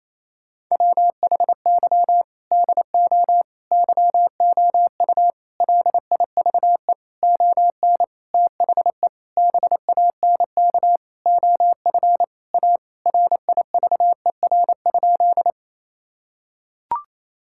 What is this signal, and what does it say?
Morse code, heard as a keyed tone, about 28 wpm, 700 Hz. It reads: WHY DO YOU LIVE ON THE BANK OF A RIVER?
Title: Sentences from Top 1000 Words - Review Part 6 - 28wpm